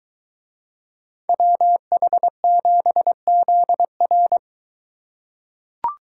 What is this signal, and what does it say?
Morse code, heard as a keyed tone, about 22 wpm, 700 Hz. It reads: WH7ZR